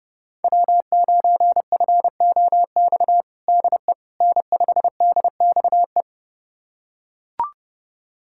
Morse code, heard as a keyed tone, about 32 wpm, 700 Hz. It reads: W9FOX DE N5DXE
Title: USA Callsign DE Callsign - 5 Characters - 30wpm